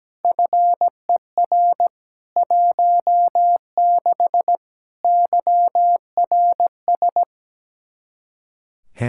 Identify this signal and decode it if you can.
Morse code, heard as a keyed tone, about 17 wpm, 700 Hz. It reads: FER 16 YRS
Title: K1USN SST QSO Practice 17wpm